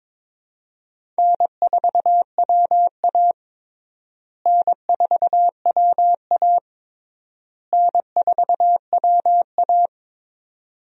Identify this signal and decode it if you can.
Morse code, heard as a keyed tone, about 22 wpm, 700 Hz. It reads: N4WA N4WA N4WA
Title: USA Call Signs - 4 Characters - Repeated 3x - 22wpm